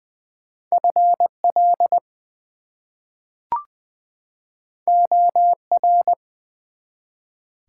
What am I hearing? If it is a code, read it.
Morse code, heard as a keyed tone, about 20 wpm, 700 Hz. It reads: FL OR
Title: US States and Canadian Provinces 20wpm